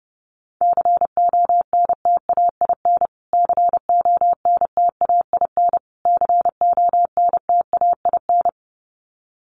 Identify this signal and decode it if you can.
Morse code, heard as a keyed tone, about 30 wpm, 700 Hz. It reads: CONTAIN CONTAIN CONTAIN
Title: Top 500 Words Repeated 30wpm